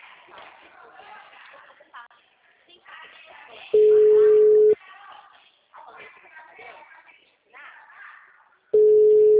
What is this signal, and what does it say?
Ringback tone (Europe, 425 Hz), with speech.